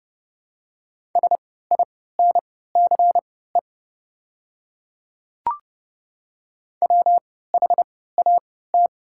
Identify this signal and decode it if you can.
Morse code, heard as a keyed tone, about 30 wpm, spaced as for 17 wpm, 700 Hz. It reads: SINCE WHAT
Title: 3x Character Spacing - Top 300 Words - 30wpm